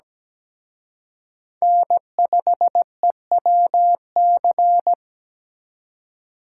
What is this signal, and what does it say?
Morse code, heard as a keyed tone, about 20 wpm, 700 Hz. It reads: N5EWC
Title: Speed Racing - USA 1x3 Call Signs 17wpm